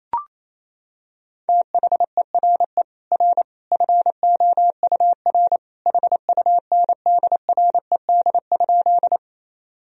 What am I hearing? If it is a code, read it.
Morse code, heard as a keyed tone, about 28 wpm, 700 Hz. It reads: THERE R FOUR HUNDRED?
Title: Sentences from Top 400 Words and Abbreviations 28wpm